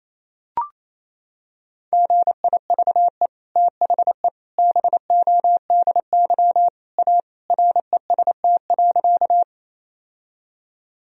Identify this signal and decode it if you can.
Morse code, heard as a keyed tone, about 27 wpm, 700 Hz. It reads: GIVE THE BODY A REST.
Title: Sentences from Top 1000 Words - Review Part 2 - 28wpm